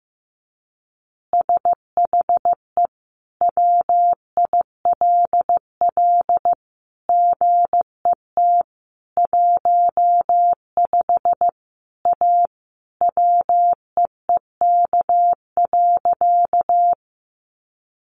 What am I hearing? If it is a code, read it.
Morse code, heard as a keyed tone, about 14 wpm, 700 Hz. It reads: SHE WILL GET 15 A WEEK.
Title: Easy Phrases 15wpm